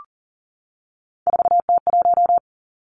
Morse code, heard as a keyed tone, about 39 wpm, 700 Hz. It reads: 4T1
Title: Three Challenging Letter-Number Combinations 40wpm